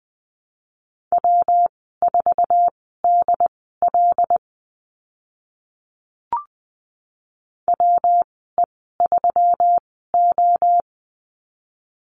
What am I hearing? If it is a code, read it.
Morse code, heard as a keyed tone, about 20 wpm, spaced as for 14 wpm, 700 Hz. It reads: W4DL WE3O
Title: 140 Lesson 23g - Letter f - Call Signs 20wpm